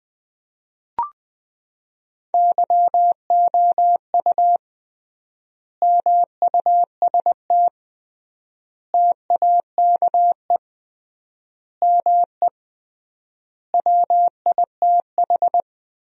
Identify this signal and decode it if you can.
Morse code, heard as a keyed tone, about 20 wpm, 700 Hz. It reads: YOU MUST TAKE ME WITH
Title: 3x Word-Spacing - Sentences from Top 200 Words - 20wpm